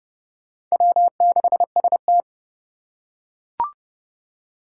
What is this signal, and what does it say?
Morse code, heard as a keyed tone, about 30 wpm, 700 Hz. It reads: W6ST
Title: USA Call Signs - 4 Characters - 3x Character Spacing - 30wpm